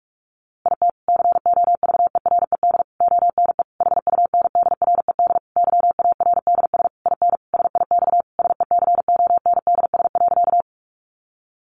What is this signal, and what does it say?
Morse code, heard as a keyed tone, about 44 wpm, 700 Hz. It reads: IT COVERED ONE HUNDRED YARDS IN SIX SECONDS.
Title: A Sound Of Thunder 45wpm